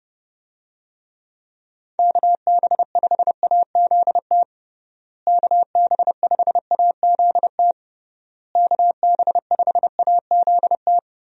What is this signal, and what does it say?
Morse code, heard as a keyed tone, about 31 wpm, 700 Hz. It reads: KB5AZT KB5AZT KB5AZT
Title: USA Call Signs - 6 Characters - Repeated 3x - 30wpm